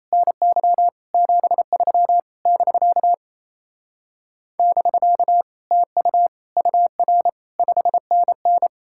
Morse code, heard as a keyed tone, about 36 wpm, 700 Hz. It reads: NY 73 <BK> <BK> TU UR 5NN